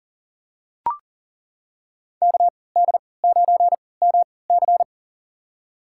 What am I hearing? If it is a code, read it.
Morse code, heard as a keyed tone, about 40 wpm, spaced as for 23 wpm, 700 Hz. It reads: KD9MC
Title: USA Call Signs - 5 Characters - 3x Character Spacing - 40wpm